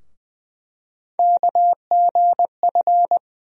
Morse code, heard as a keyed tone, about 20 wpm, 700 Hz. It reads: KGF
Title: USA Call Sign Postfixes - 3 Letters 20wpm